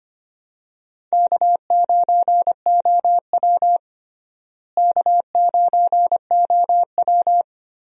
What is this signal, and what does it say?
Morse code, heard as a keyed tone, about 25 wpm, 700 Hz. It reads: K9OW K9OW
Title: USA Call Signs - 4 Characters - Repeated 3x - 25wpm